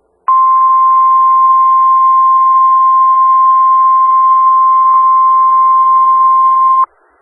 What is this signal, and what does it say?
Emergency Alert System (SAME) tones (the 1050 Hz attention tone). It is heard from 0.2 to 6.9 s.
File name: Skywarn/2013/2013-03-18-Newton County WA4ASI Repeater